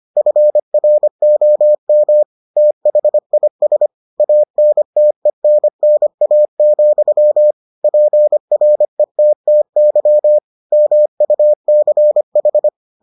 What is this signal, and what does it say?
Morse code, heard as a keyed tone, about 25 wpm, 600 Hz. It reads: FROM THIS ANTENNA, PRETTY MUCH